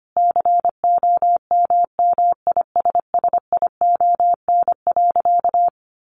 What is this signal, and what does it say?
Morse code, heard as a keyed tone, about 25 wpm, 700 Hz. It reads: COMMISSION.
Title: Sentences from Top 1400 Words 25wpm